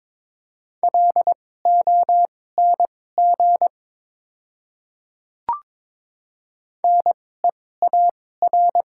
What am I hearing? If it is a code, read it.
Morse code, heard as a keyed tone, about 22 wpm, spaced as for 16 wpm, 700 Hz. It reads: LONG NEAR
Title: 2x Character Spacing - Top 200 Words - 22wpm